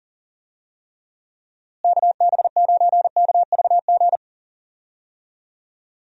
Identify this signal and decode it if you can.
Morse code, heard as a keyed tone, about 40 wpm, 700 Hz. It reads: KB9KVG
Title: USA Call Signs - 6 Characters - Repeated 3x - 40wpm